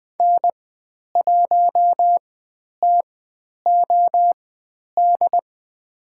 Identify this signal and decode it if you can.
Morse code, heard as a keyed tone, about 20 wpm, spaced as for 10 wpm, 700 Hz. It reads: N1TOD
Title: 095 Lesson 16g - Letter c - Call Signs 20wpm